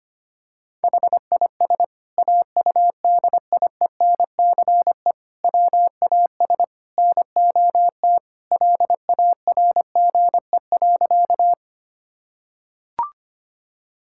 Morse code, heard as a keyed tone, about 25 wpm, 700 Hz. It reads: HIS AUDIENCE WAS NOT LARGE.
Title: Sentences from Top 1000 Words - Review Part 5 - 25wpm